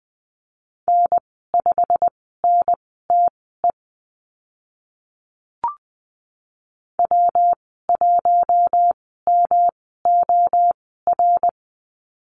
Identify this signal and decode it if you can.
Morse code, heard as a keyed tone, about 20 wpm, spaced as for 15 wpm, 700 Hz. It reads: N5NTE W1MOR
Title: 147 Lesson 24g - Letter y - Call Signs 20wpm